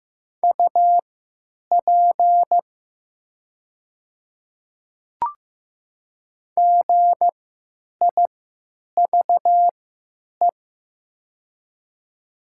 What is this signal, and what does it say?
Morse code, heard as a keyed tone, about 15 wpm, spaced as for 9 wpm, 700 Hz. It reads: UP GIVE
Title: 3x Character Spacing - Top 200 Words - 15wpm